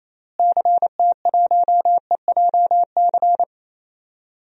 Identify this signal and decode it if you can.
Morse code, heard as a keyed tone, about 28 wpm, 700 Hz. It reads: CT1EJC
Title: DXCC-100 Call Signs - 6 Characters - Repeated 2x - 28wpm